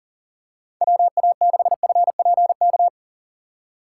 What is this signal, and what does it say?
Morse code, heard as a keyed tone, about 37 wpm, 700 Hz. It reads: WA6FPK
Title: USA Call Signs - 6 Characters - 4x Character Spacing - 40wpm